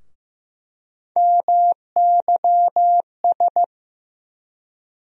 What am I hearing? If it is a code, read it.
Morse code, heard as a keyed tone, about 15 wpm, 700 Hz. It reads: MYS
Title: USA Call Sign Postfixes - 3 Letters 15wpm